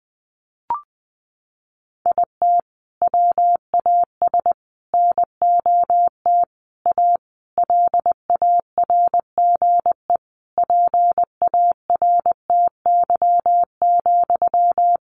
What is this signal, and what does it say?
Morse code, heard as a keyed tone, about 20 wpm, 700 Hz. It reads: IT WAS NOT A LARGE PARTY,
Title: Sentences from Top 1400 Words 20wpm